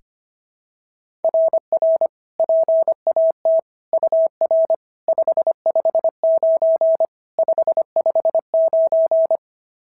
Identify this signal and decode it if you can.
Morse code, heard as a keyed tone, about 22 wpm, 650 Hz. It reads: RR PAT UR 559 559
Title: SOTA QSO Practice 25wpm